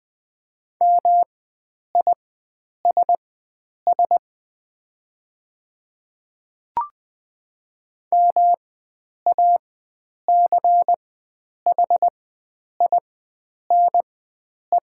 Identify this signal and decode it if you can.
Morse code, heard as a keyed tone, about 20 wpm, spaced as for 9 wpm, 700 Hz. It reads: MISS MACHINE
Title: 4x Character Spacing - Top 500 Words - 20wpm